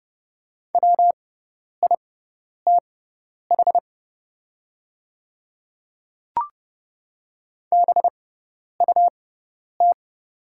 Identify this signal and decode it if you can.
Morse code, heard as a keyed tone, about 30 wpm, spaced as for 10 wpm, 700 Hz. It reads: WITH BUT
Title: 6x Character Spacing - Top 200 Words - 30wpm